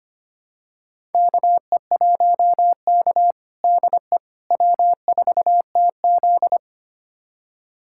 Morse code, heard as a keyed tone, about 25 wpm, 700 Hz. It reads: KE1K DE W4TZ
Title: USA Callsign DE Callsign - 4 Characters - 25wpm